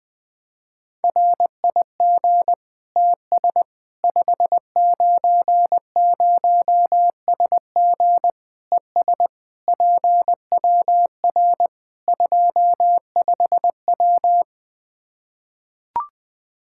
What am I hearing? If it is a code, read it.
Morse code, heard as a keyed tone, about 20 wpm, 700 Hz. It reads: RIG TS 590SG ES PWR 25W
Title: K1USN SST QSO Practice 20wpm